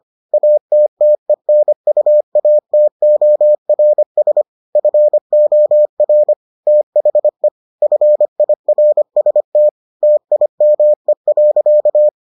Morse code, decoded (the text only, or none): ATTENUATORS FOR THE FIRST TIME.